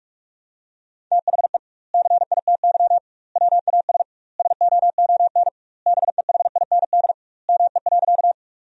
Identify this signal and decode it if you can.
Morse code, heard as a keyed tone, about 46 wpm, 700 Hz. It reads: THE CITY WAS SOON BEHIND ME.
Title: 2x Word-Spacing - Sentences from Top 500 Words - 45wpm